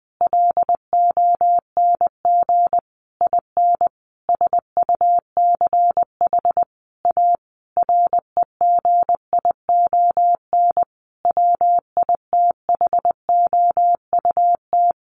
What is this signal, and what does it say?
Morse code, heard as a keyed tone, about 20 wpm, 700 Hz. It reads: LONG IN SUCH A REGION WITHOUT
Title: Sentences from Top 1500 Words 20wpm